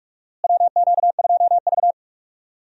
Morse code, heard as a keyed tone, about 43 wpm, 700 Hz. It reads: WQ2V
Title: USA Call Signs - 4 Characters - Repeated 2x - 45wpm